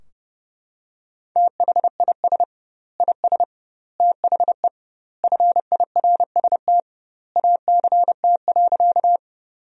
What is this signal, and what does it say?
Morse code, heard as a keyed tone, about 30 wpm, 700 Hz. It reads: THIS IS THE FIRST ACT.
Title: 2x Word-Spacing - Sentences from Top 200 Words - 30wpm